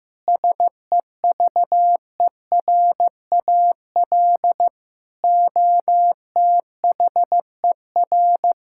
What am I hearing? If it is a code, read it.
Morse code, heard as a keyed tone, about 16 wpm, 700 Hz. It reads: SEVERAL OTHER
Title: Sentences from Top 1000 Words - Review Part 6 - 15wpm